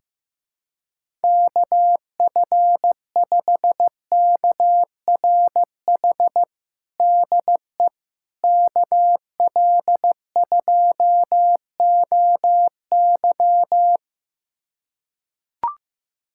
Morse code, heard as a keyed tone, about 15 wpm, 700 Hz. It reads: KF5KRH DE KL2OY
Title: USA Callsign DE Callsign 15wpm